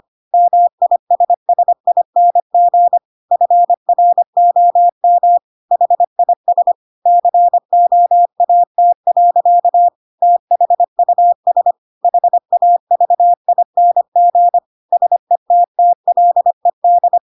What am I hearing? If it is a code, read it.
Morse code, heard as a keyed tone, about 25 wpm, 700 Hz. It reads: MISSING FROM HIS COAT. THUS HAVING SETTLED
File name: Pinocchio / Pinocchio ch2 25 25 ik1hge